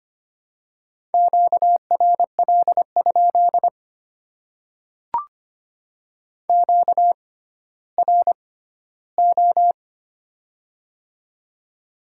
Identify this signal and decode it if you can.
Morse code, heard as a keyed tone, about 25 wpm, spaced as for 14 wpm, 700 Hz. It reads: QRL? QRO
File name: Top 90 QSO Elements - 6x Character Spacing - 25wpm